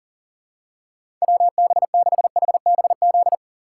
Morse code, heard as a keyed tone, about 40 wpm, 700 Hz. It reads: WB6HBZ
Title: USA Call Signs - 6 Characters - 2x Character Spacing - 40wpm